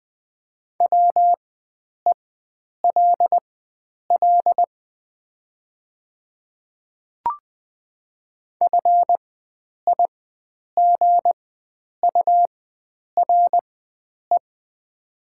Morse code, heard as a keyed tone, about 20 wpm, spaced as for 9 wpm, 700 Hz. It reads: WELL FIGURE
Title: 4x Character Spacing - Top 500 Words - 20wpm